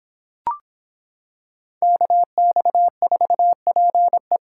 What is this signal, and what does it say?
Morse code, heard as a keyed tone, about 25 wpm, 700 Hz. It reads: KX4PE